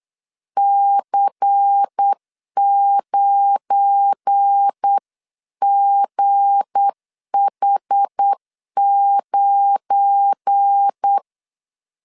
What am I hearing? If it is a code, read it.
Morse code, heard as a keyed tone, about 8 wpm, 800 Hz. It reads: C9GH9
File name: LEZIONI / 7 GF9Z / 7 GF9 8wpm 3545